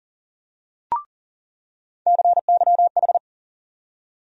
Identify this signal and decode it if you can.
Morse code, heard as a keyed tone, about 39 wpm, 700 Hz. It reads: CYH
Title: USA Call Sign Postfixes - 3 Letters 40wpm